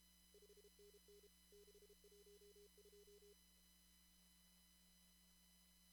Morse code, heard as a keyed tone, about 32 wpm, 410 Hz. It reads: 5NN 612
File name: CW / kt5j wpx cw-op-k5pi